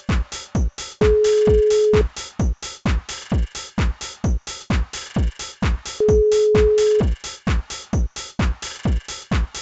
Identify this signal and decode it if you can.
Ringback tone (Europe, 425 Hz).